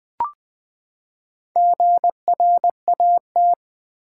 Morse code, heard as a keyed tone, about 20 wpm, 700 Hz. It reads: GRAT
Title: Build A Word - 4 Letters At A Time - 8-Letter Words - 20wpm